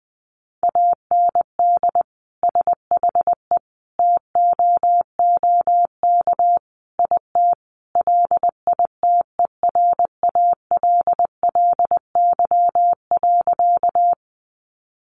Morse code, heard as a keyed tone, about 20 wpm, 700 Hz. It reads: AND SHE TOOK IT LITERALLY.
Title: Sentences from Top 1400 Words 20wpm